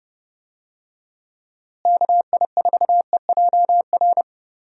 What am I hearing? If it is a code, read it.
Morse code, heard as a keyed tone, about 30 wpm, 700 Hz. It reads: KI4EJR
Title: USA Call Signs - 6 Characters - Repeated 3x - 30wpm